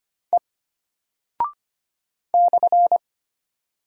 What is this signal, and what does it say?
Morse code, heard as a keyed tone, about 25 wpm, 700 Hz. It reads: E /
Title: Single Character (Letter:Number:Punctuation:Prosign) Alt 25wpm